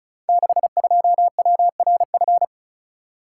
Morse code, heard as a keyed tone, about 36 wpm, 690 Hz. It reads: 62WRF
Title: Code Groups - 5 Letter:Numbers - 35wpm